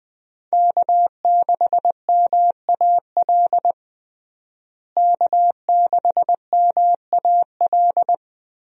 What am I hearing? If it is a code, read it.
Morse code, heard as a keyed tone, about 20 wpm, 700 Hz. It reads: K6MAL K6MAL
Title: USA Call Signs - 5 Characters - Repeated 3x - 20wpm